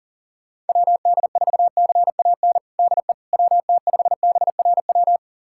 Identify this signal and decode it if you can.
Morse code, heard as a keyed tone, about 40 wpm, 700 Hz. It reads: WD4CAN DE WT5BRW
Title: USA Callsign DE Callsign 40wpm